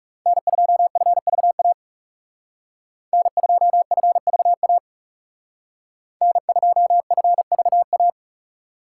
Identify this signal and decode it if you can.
Morse code, heard as a keyed tone, about 36 wpm, 700 Hz. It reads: N2FVA N2FVA N2FVA
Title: Speed Racing - USA 1x3 Call Signs 30wpm